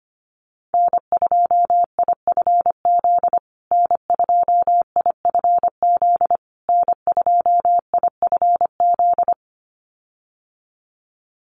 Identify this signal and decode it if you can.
Morse code, heard as a keyed tone, about 25 wpm, 700 Hz. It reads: N2IFZ N2IFZ N2IFZ
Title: USA Call Signs Repeated 3x 25wpm